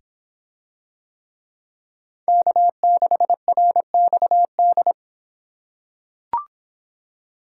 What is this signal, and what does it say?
Morse code, heard as a keyed tone, about 26 wpm, 700 Hz. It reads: K6RXD